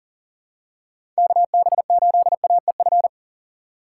Morse code, heard as a keyed tone, about 38 wpm, 700 Hz. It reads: KB8AEF
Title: USA Call Signs - 6 Characters - 3x Character Spacing - 40wpm